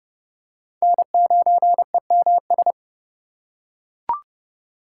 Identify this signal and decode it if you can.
Morse code, heard as a keyed tone, about 30 wpm, 700 Hz. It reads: N9EMS
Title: USA Call Signs - 6x Character Spacing - 30wpm